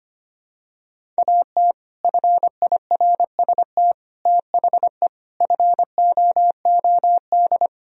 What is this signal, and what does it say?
Morse code, heard as a keyed tone, about 25 wpm, 700 Hz. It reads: AT FIRST THE FOOD
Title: Sentences from Top 1000 Words - Review Part 1 - 25wpm